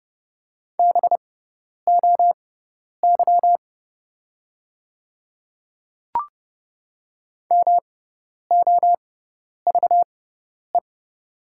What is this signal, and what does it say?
Morse code, heard as a keyed tone, about 30 wpm, spaced as for 10 wpm, 690 Hz. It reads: BOY MOVE
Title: 6x Character Spacing - Top 200 Words - 30wpm